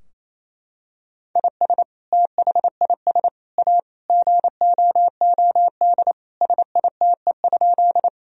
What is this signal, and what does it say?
Morse code, heard as a keyed tone, about 28 wpm, 700 Hz. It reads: IS THIS A GOOD SITE?
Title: Sentences from Top 1000 Words - Review Part 4 - 28wpm